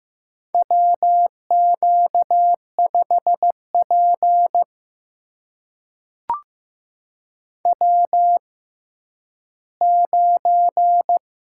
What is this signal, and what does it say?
Morse code, heard as a keyed tone, about 15 wpm, spaced as for 10 wpm, 700 Hz. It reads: WQ5P W9
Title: USA Call Signs - 4 Characters - 6x Character Spacing - 15wpm